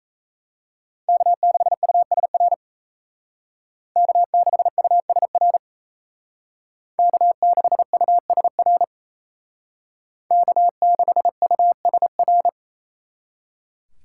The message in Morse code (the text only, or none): K6USR K6USR K6USR K6USR